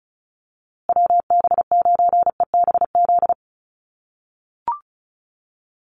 Morse code, heard as a keyed tone, about 34 wpm, 700 Hz. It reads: WB9EBZ
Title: USA Call Signs Repeated 3x 35wpm